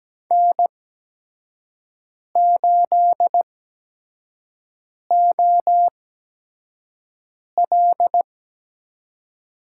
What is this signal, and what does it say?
Morse code, heard as a keyed tone, about 17 wpm, spaced as for 5 wpm, 700 Hz. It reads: N8OL
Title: USA Call Signs - 8x Character Spacing - 17wpm